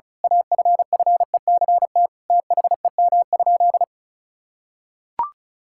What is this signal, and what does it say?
Morse code, heard as a keyed tone, about 36 wpm, 700 Hz. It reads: AFFECT THEM?
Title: Sentences from Top 1000 Words - Review Part 6 - 35wpm